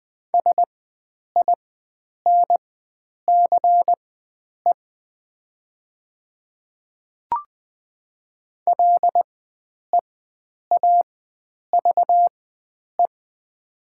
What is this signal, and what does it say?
Morse code, heard as a keyed tone, about 20 wpm, spaced as for 9 wpm, 700 Hz. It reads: SINCE LEAVE
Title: 4x Character Spacing - Top 500 Words - 20wpm